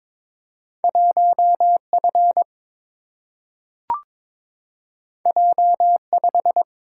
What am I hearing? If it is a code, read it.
Morse code, heard as a keyed tone, about 22 wpm, 700 Hz. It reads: 1 F J 5
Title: Single-Letter Single-Number 22wpm